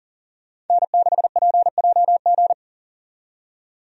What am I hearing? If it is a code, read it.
Morse code, heard as a keyed tone, about 40 wpm, 700 Hz. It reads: N6PJG
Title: USA Call Signs - 5 Characters - 40wpm